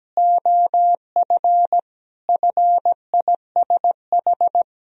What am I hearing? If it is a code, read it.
Morse code, heard as a keyed tone, about 17 wpm, 700 Hz. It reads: OF FISH
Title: Sentences from Top 1000 Words - Review Part 1 - 17wpm